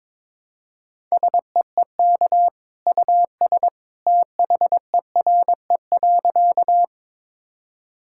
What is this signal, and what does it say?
Morse code, heard as a keyed tone, about 22 wpm, 700 Hz. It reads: SEEK US THERE.
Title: Sentences from Top 1600 Words 22wpm